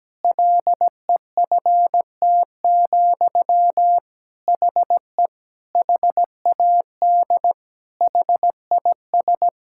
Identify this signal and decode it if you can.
Morse code, heard as a keyed tone, about 17 wpm, 700 Hz. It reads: LEFT, HE HAD HIS
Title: Sentences from Top 1000 Words - Review Part 5 - 17wpm